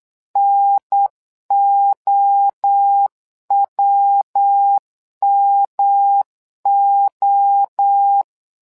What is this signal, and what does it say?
Morse code, heard as a keyed tone, about 8 wpm, 800 Hz. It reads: NOWMO
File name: LEZIONI / 3 MWRO / 3 MWRO 8wpm 2606